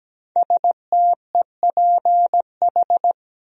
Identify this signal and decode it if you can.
Morse code, heard as a keyed tone, about 17 wpm, 700 Hz. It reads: STEPH
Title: USA Common Names - 17wpm